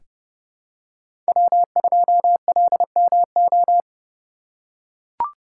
Morse code, heard as a keyed tone, about 30 wpm, 700 Hz. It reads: W2LMO
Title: USA Call Signs - 5 Characters - 3x Character Spacing - 30wpm